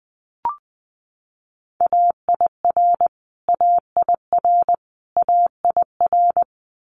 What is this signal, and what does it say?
Morse code, heard as a keyed tone, about 20 wpm, 700 Hz. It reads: AIR AIR AIR
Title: Top 200 Words Repeated 20wpm